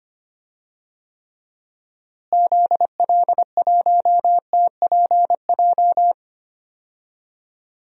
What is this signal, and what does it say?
Morse code, heard as a keyed tone, about 25 wpm, 700 Hz. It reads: ZL1TPJ
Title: DXCC-100 Call Signs - 6 Characters - Repeated 3x - 25wpm